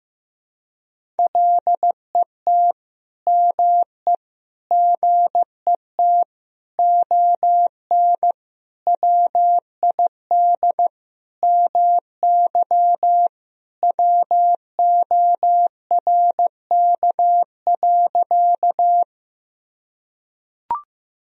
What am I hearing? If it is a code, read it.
Morse code, heard as a keyed tone, about 15 wpm, 700 Hz. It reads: LET ME GET ON WID MY WORK.
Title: Sentences from Top 300 Words and Abbreviations 15wpm